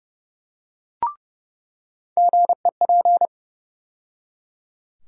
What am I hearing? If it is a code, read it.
Morse code, heard as a keyed tone, about 32 wpm, 700 Hz. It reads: GEP
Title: USA Call Sign Postfixes - 3 Letters 30wpm